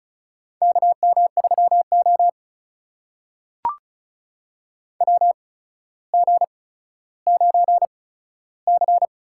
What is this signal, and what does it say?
Morse code, heard as a keyed tone, about 34 wpm, spaced as for 15 wpm, 700 Hz. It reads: KM3O WG9C
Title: USA Call Signs - 4 Characters - 8x Character Spacing - 35wpm